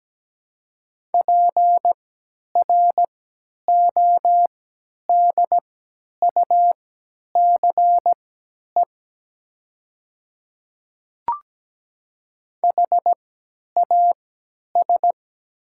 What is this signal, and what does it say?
Morse code, heard as a keyed tone, about 17 wpm, spaced as for 10 wpm, 700 Hz. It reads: PRODUCE HAS
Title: 3x Character Spacing - Top 500 Words - 17wpm